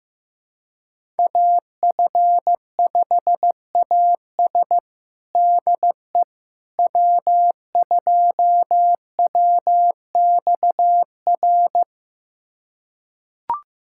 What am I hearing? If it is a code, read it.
Morse code, heard as a keyed tone, about 15 wpm, 700 Hz. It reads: AF5AS DE W2WXR
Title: USA Callsign DE Callsign - 5 Characters - 15wpm